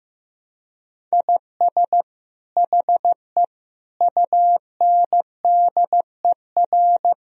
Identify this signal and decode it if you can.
Morse code, heard as a keyed tone, about 15 wpm, 700 Hz. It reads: IS HE UNDER
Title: Sentences from Top 1000 Words - Review Part 1 - 15wpm